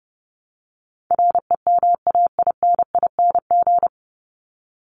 Morse code, heard as a keyed tone, about 30 wpm, 700 Hz. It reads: REMAINING
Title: ING Suffix - Top 100 Words with Base Words 30wpm